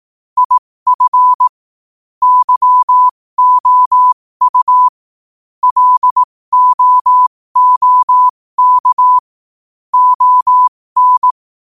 Morse code, heard as a keyed tone, about 18 wpm, spaced as for 15 wpm, 1000 Hz. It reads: IF YOU LOOK ON
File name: qotdfiles / 15wpm